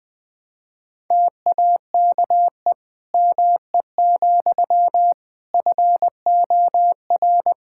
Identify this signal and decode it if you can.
Morse code, heard as a keyed tone, about 20 wpm, 700 Hz. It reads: TAKE ME, FOR
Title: Sentences from Top 1000 Words - Review Part 5 - 20wpm